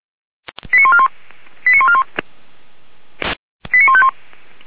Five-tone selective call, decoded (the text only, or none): 89141 89141 89151